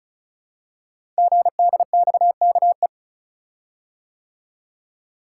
Morse code, heard as a keyed tone, about 36 wpm, 700 Hz. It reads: GDXKE
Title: Code Groups - 5 Letters - 35wpm